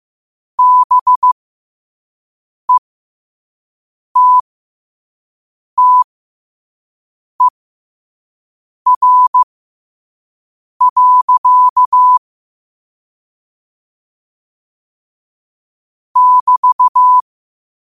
Morse code, heard as a keyed tone, about 15 wpm, spaced as for 5 wpm, 1000 Hz. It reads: BETTER. <BT>
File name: qotdfiles / 5wpm